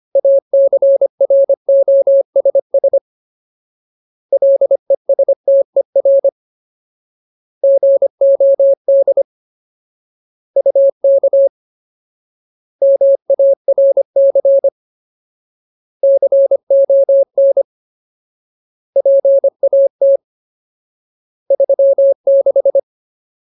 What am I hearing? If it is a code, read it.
Morse code, heard as a keyed tone, about 25 wpm, 550 Hz. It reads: ACROSS LESTER GOD UK MARC CON PAT 36
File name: Ultimate Word List / 25WPM Ultimate